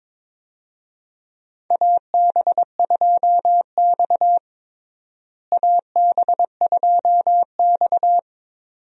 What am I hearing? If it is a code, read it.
Morse code, heard as a keyed tone, about 22 wpm, 700 Hz. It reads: AB2X AB2X
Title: USA Call Signs - 4 Characters - Repeated 2x - 22wpm